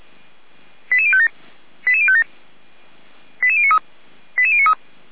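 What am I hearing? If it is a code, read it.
Five-tone selective call (ZVEI), heard as quick runs of five tones, tones of about 70 ms. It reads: 88057 88057 88093 88093
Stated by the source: ZVEI five-tone call received by SDR, recorded by Feuerwehr Orlamünde fire brigade.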